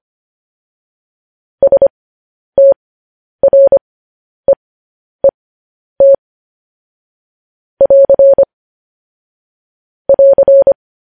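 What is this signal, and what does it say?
Morse code, heard as a keyed tone, about 25 wpm, spaced as for 10 wpm, 550 Hz. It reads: STREET <AR> <AR>
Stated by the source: Morse news headlines sent at 10 words per minute.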